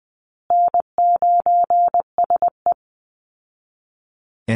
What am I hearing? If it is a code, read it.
Morse code, heard as a keyed tone, about 20 wpm, 700 Hz. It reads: N9SE
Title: Easy Call Signs Repeated 2x - 20wpm